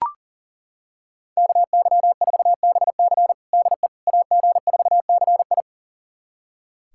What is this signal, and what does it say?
Morse code, heard as a keyed tone, about 40 wpm, 700 Hz. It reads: KY4BC DE AG4CI
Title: USA Callsign DE Callsign - 5 Characters - 40wpm